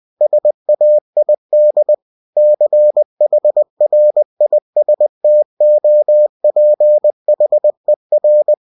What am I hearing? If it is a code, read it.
Morse code, heard as a keyed tone, about 20 wpm, 600 Hz. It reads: SAID CHRISTOPHER
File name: cw-books / winnie-the-pooh / winnie-20wpm-page-0033